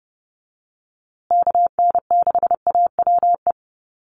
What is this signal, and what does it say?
Morse code, heard as a keyed tone, about 30 wpm, 700 Hz. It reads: KN6AWE
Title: USA Call Signs Repeated 3x 30wpm